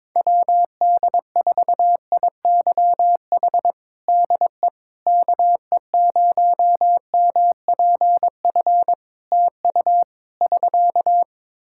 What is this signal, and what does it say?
Morse code, heard as a keyed tone, about 22 wpm, 700 Hz. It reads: WD4IYH DE KE0MPF TU <SK>
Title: Practice QSO - Exchange 4 - Ending - 22wpm